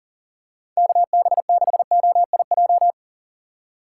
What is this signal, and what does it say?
Morse code, heard as a keyed tone, about 38 wpm, 700 Hz. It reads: KB6OIJ
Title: USA Call Signs - 6 Characters - 4x Character Spacing - 40wpm